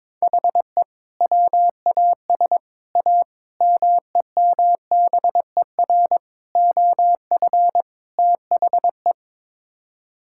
Morse code, heard as a keyed tone, about 22 wpm, 700 Hz. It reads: HE WAS A MEMBER OF THE
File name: practice / 22wpm / Sets of 6 Words - 22wpm